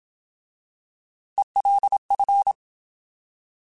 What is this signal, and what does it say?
Morse code, heard as a keyed tone, about 27 wpm, 800 Hz. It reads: ELF
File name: Alphabetical / Alphabetical mp3s / Es @25